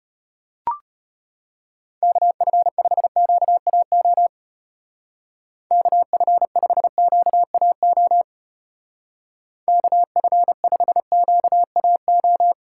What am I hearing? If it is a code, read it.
Morse code, heard as a keyed tone, about 32 wpm, 700 Hz. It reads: KF5QAO KF5QAO KF5QAO